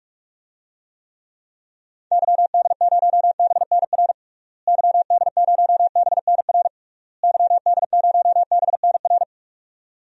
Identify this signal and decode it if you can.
Morse code, heard as a keyed tone, about 46 wpm, 700 Hz. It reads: YD0BNR YD0BNR YD0BNR
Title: DXCC-100 Call Signs - 6 Characters - Repeated 3x - 45wpm